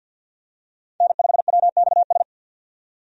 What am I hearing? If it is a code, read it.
Morse code, heard as a keyed tone, about 46 wpm, 700 Hz. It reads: N5WXS
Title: USA Call Signs - 5 Characters - 50wpm